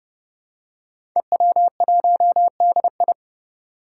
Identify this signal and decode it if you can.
Morse code, heard as a keyed tone, about 30 wpm, 700 Hz. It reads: EW1DI
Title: Five Challenging Letter-Number Combinations 30wpm